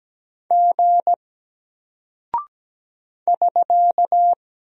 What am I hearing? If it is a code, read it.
Morse code, heard as a keyed tone, about 17 wpm, 700 Hz. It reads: G <SK>
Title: Single Character (Letter:Number:Punctuation:Prosign) 17wpm